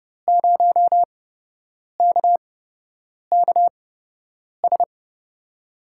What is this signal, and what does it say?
Morse code, heard as a keyed tone, about 30 wpm, spaced as for 8 wpm, 700 Hz. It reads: 0KKS
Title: USA Call Signs - 6 Characters - 8x Character Spacing - 30wpm